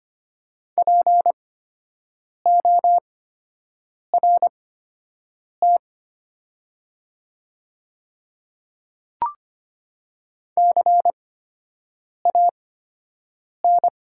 Morse code, heard as a keyed tone, about 25 wpm, spaced as for 7 wpm, 700 Hz. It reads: PORT CAN
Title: 8x Character Spacing - Top 300 Words - 25wpm